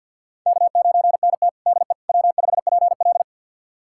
Morse code, heard as a keyed tone, about 51 wpm, 700 Hz. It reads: K9NT DE W5PL
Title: USA Callsign DE Callsign - 4 Characters - 50wpm